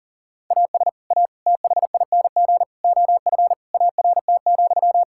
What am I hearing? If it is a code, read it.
Morse code, heard as a keyed tone, about 40 wpm, 700 Hz. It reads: AS A THING OF ART,
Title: Sentences from Top 1000 Words - Review Part 6 - 40wpm